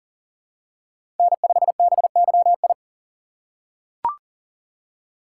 Morse code, heard as a keyed tone, about 40 wpm, 700 Hz. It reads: N5BYI
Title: USA Call Signs - 4x Character Spacing - 40wpm